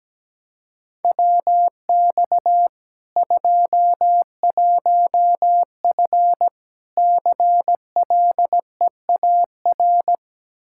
Morse code, heard as a keyed tone, about 17 wpm, 700 Hz. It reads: WX 21F CLEAR
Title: K1USN SST QSO Practice 17wpm